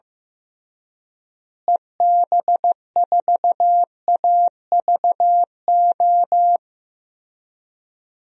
Morse code, heard as a keyed tone, about 15 wpm, 700 Hz. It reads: EB4AVO
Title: DXCC-100 Call Signs - 6 Characters - Repeated 3x - 15wpm